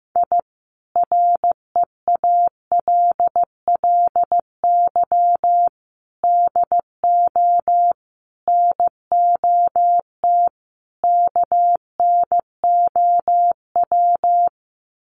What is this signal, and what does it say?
Morse code, heard as a keyed tone, about 15 wpm, 700 Hz. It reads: I REALLY DO NOT KNOW
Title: Sets of 5 Words Encore 15wpm